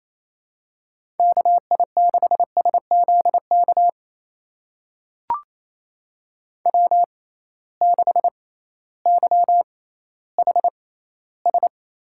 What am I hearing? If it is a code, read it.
Morse code, heard as a keyed tone, about 28 wpm, spaced as for 15 wpm, 700 Hz. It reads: KI6SZK W6YHS
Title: USA Call Signs - 6x Character Spacing - 28wpm